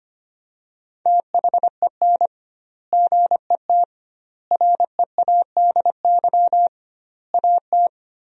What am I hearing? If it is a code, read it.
Morse code, heard as a keyed tone, about 25 wpm, 700 Hz. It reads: THEN GET READY AT
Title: 2x Word-Spacing - Sentences from Top 400 Words - 25wpm